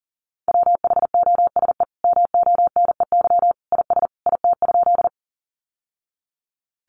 Morse code, heard as a keyed tone, about 40 wpm, 700 Hz. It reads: WHOSE MONEY IS IT?
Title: Sentences from Top 1200 Words 40wpm